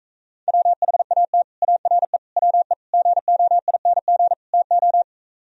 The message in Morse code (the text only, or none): WHAT ARE WE GOING TO